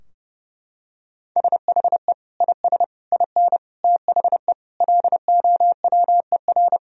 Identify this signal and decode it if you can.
Morse code, heard as a keyed tone, about 32 wpm, 700 Hz. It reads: SHE IS IN THE LOWER
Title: Sentences from Top 1000 Words - Review Part 4 - 30wpm